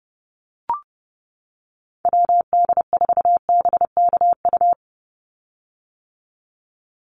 Morse code, heard as a keyed tone, about 30 wpm, 700 Hz. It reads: WD4BKU
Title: USA Call Signs 30wpm